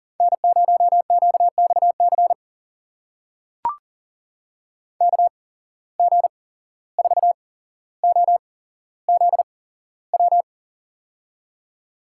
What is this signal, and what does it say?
Morse code, heard as a keyed tone, about 40 wpm, spaced as for 16 wpm, 700 Hz. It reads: N0QXC KG4OZW
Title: USA Call Signs - 8x Character Spacing - 40wpm